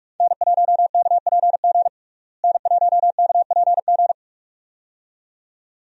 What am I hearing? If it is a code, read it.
Morse code, heard as a keyed tone, about 44 wpm, 700 Hz. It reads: N1KPG N1KPG
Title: USA Call Signs - 5 Characters - Repeated 2x - 45wpm